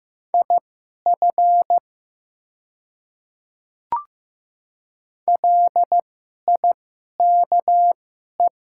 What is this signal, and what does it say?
Morse code, heard as a keyed tone, about 15 wpm, spaced as for 11 wpm, 700 Hz. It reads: IF LIKE
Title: 2x Character Spacing - Top 100 Words - 15wpm